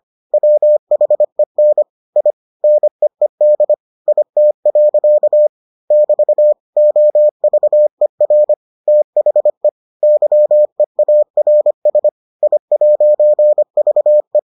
Morse code, heard as a keyed tone, about 25 wpm, 600 Hz. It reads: WHEN I NEED IT. <BT> OVER THE YEARS I'VE